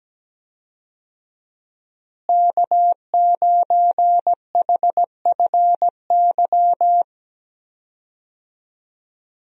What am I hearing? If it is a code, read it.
Morse code, heard as a keyed tone, about 17 wpm, 700 Hz. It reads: K9HFY
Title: USA Call Signs - 5 Characters - Repeated 2x - 17wpm